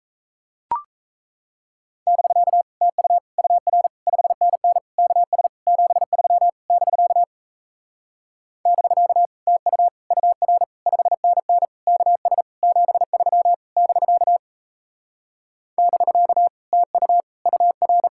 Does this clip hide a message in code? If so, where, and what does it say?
Yes: Morse code from 2.1 to 18.1 s, heard as a keyed tone, about 37 wpm, 700 Hz. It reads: <BK> TU UR 5NN KS 73 <BK> <BK> TU UR 5NN KS 73 <BK> <BK> TU UR